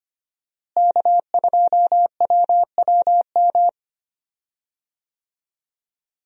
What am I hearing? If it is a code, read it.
Morse code, heard as a keyed tone, about 25 wpm, 700 Hz. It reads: K2WWM
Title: USA Call Signs - 5 Characters - Repeated 2x - 25wpm